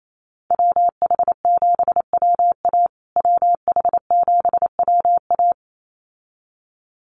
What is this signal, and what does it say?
Morse code, heard as a keyed tone, about 28 wpm, 700 Hz. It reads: WH7WA WH7WA
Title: USA Call Signs Repeated 2x 28wpm